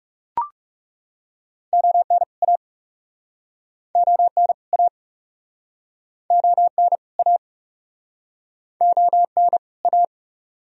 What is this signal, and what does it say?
Morse code, heard as a keyed tone, about 44 wpm, 700 Hz. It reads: ON A ON A ON A ON A